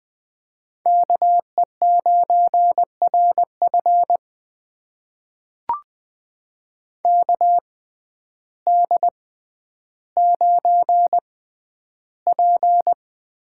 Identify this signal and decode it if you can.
Morse code, heard as a keyed tone, about 20 wpm, spaced as for 11 wpm, 700 Hz. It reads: KE9RF KD9P
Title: USA Call Signs - 5 Characters - 6x Character Spacing - 20wpm